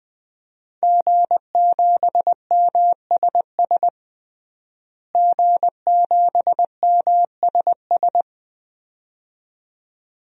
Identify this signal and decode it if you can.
Morse code, heard as a keyed tone, about 21 wpm, 700 Hz. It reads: G7MSS G7MSS
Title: DXCC-100 Call Signs - 5 Characters - Repeated 2x - 20wpm